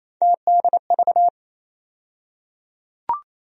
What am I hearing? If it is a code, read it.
Morse code, heard as a keyed tone, about 28 wpm, 700 Hz. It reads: TDV
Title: USA Call Sign Postfixes - 3 Letters 28wpm